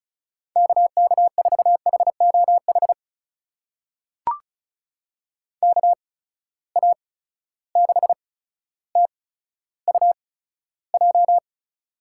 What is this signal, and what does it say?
Morse code, heard as a keyed tone, about 36 wpm, spaced as for 15 wpm, 700 Hz. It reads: KK4HOH KA6TUJ
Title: USA Call Signs - 6 Characters - 8x Character Spacing - 35wpm